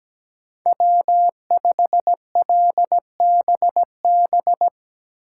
Morse code, heard as a keyed tone, about 17 wpm, 700 Hz. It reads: W5LBB
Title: USA Call Signs - 4x Character Spacing - 17wpm